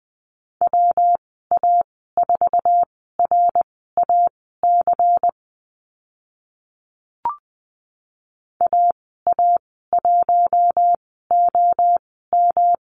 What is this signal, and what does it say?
Morse code, heard as a keyed tone, about 20 wpm, spaced as for 14 wpm, 700 Hz. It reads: WA4RAC AA1OM
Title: 167 Lesson 27g - Number 7 - Call Signs 20wpm